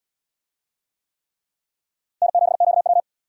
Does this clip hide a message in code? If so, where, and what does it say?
Yes: Morse code from 2.2 to 3.0 s, heard as a keyed tone, about 72 wpm, 700 Hz. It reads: N7QW